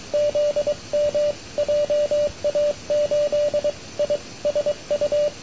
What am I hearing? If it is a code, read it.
Morse code, heard as a keyed tone, about 22 wpm, 590 Hz, with radio noise. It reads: ZM JA8ISU
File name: n6tr / VY2ZMLongpath2008Nov23-2100UTCatJA8ISU